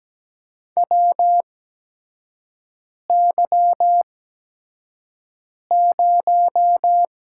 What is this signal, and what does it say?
Morse code, heard as a keyed tone, about 17 wpm, spaced as for 5 wpm, 700 Hz. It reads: WY0
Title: USA Call Signs - 6 Characters - 8x Character Spacing - 17wpm